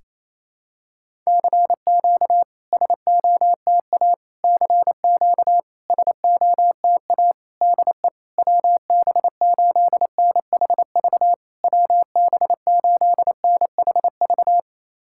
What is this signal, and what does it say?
Morse code, heard as a keyed tone, about 28 wpm, 700 Hz. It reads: CQ SOTA CQ SOTA DE WB8NHV WB8NHV
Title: SOTA QSO Practice 28wpm